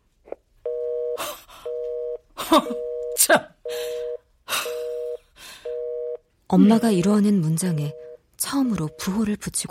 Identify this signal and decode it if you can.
Telephone call progress tone (North America), heard as a telephone network's tone, about 480, 620 Hz, with speech. Busy tone.